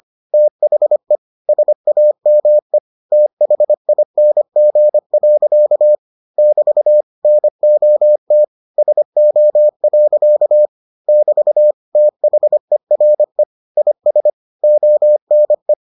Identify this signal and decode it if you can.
Morse code, heard as a keyed tone, about 25 wpm, 600 Hz. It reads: THE SAME THING. <BT> NOT SO. <BT> THERE IS ONE